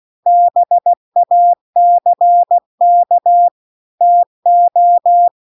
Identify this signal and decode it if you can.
Morse code, heard as a keyed tone, about 16 wpm, 700 Hz. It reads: BACK TO
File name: CW-The-Big-Twoheaded-River-2-16WPM